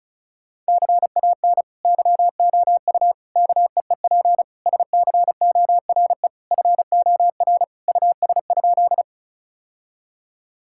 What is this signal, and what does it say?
Morse code, heard as a keyed tone, about 36 wpm, 700 Hz. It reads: CAN YOU KEEP SCORE FOR US?
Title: Sentences from Top 1600 Words 35wpm